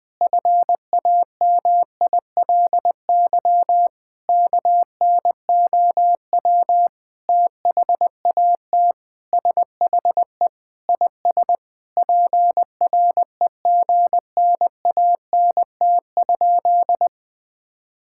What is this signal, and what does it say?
Morse code, heard as a keyed tone, about 20 wpm, 700 Hz. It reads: FAMILY KNOW THAT SHE IS PREGNANT?
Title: Sentences from Top 1000 Words - Review Part 6 - 20wpm